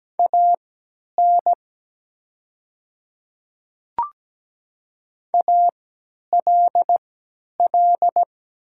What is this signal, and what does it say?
Morse code, heard as a keyed tone, about 17 wpm, spaced as for 10 wpm, 700 Hz. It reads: AN ALL
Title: 3x Character Spacing - Top 100 Words - 17wpm